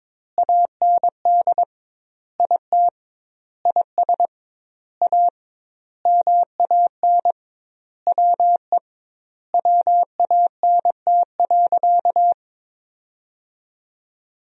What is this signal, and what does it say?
Morse code, heard as a keyed tone, about 22 wpm, 700 Hz. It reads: AND IT IS A MAN WE WANT.
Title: 2x Word-Spacing - Sentences from Top 200 Words - 22wpm